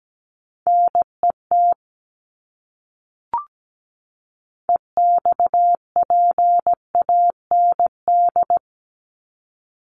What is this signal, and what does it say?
Morse code, heard as a keyed tone, about 17 wpm, 700 Hz. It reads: NET EXPAND
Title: Top 1251-1500 Words 17wpm